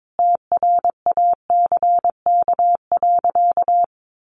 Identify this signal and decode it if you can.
Morse code, heard as a keyed tone, about 22 wpm, 700 Hz. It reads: TRACK.
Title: Sentences from Top 1100 Words 22wpm